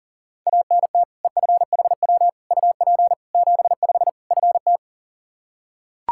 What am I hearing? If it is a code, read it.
Morse code, heard as a keyed tone, about 40 wpm, 700 Hz. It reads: ANT EFHW UP 75 FT
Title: Practice QSO - Exchange 2 - Info - 40wpm